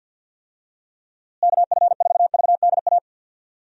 Morse code, heard as a keyed tone, about 51 wpm, 700 Hz. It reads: KF4VDA
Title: USA Call Signs - 6 Characters - Repeated 2x - 50wpm